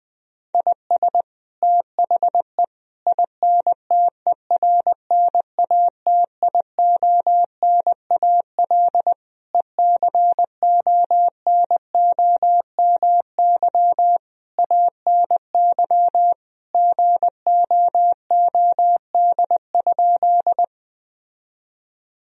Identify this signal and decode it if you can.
Morse code, heard as a keyed tone, about 20 wpm, 700 Hz. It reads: IS THE INTERNATIONAL ECONOMY ANY GOOD?
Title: Sentences from Top 1000 Words - Review Part 4 - 20wpm